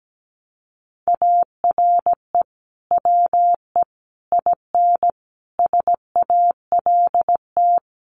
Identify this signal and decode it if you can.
Morse code, heard as a keyed tone, about 17 wpm, 700 Hz. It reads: ARE WE IN SALT
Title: Sentences from Top 1100 Words 17wpm